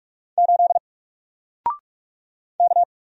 Morse code, heard as a keyed tone, about 44 wpm, 700 Hz. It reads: 8 K
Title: Single Letter-Number Alt 45wpm